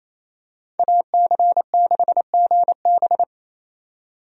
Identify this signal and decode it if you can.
Morse code, heard as a keyed tone, about 28 wpm, 700 Hz. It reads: AC6GB
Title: USA Call Signs - 5 Characters - 28wpm